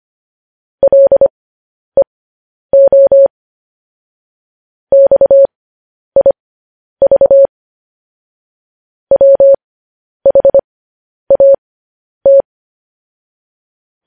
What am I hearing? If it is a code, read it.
Morse code, heard as a keyed tone, about 25 wpm, spaced as for 10 wpm, 550 Hz. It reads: LEO XIV WHAT
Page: news-headlines-in-morse-code-at-10-wpm